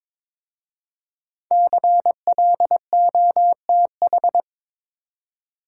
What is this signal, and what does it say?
Morse code, heard as a keyed tone, about 22 wpm, 700 Hz. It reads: CLOTH
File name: 5 Letter Words - First 3 Letters Given - 22wpm